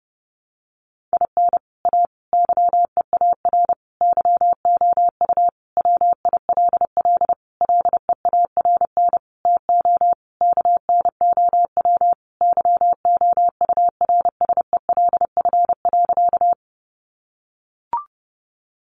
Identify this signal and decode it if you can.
Morse code, heard as a keyed tone, about 30 wpm, 700 Hz. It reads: IN A YEAR YOU WILL LEARN TO KNOW YOURSELF.
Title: Sentences from Top 600 Words 30wpm